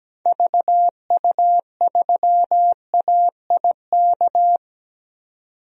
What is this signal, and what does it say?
Morse code, heard as a keyed tone, about 17 wpm, 700 Hz. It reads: VU3AIK
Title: DXCC-100 Call Signs - 6 Characters - 17wpm